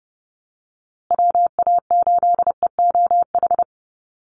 Morse code, heard as a keyed tone, about 30 wpm, 700 Hz. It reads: WA8EOH
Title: USA Call Signs Repeated 2x 30wpm